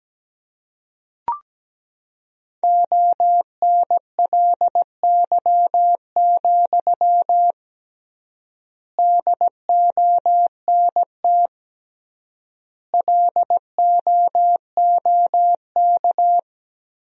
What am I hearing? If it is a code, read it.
Morse code, heard as a keyed tone, about 17 wpm, 700 Hz. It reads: ONLY, DONT LOOK
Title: 3x Word-Spacing - Sentences from Top 300 Words - 17wpm